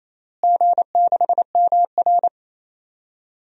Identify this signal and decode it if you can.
Morse code, heard as a keyed tone, about 27 wpm, 700 Hz. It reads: G6MR
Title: DXCC-100 Call Signs - 4 Characters - Repeated 2x - 28wpm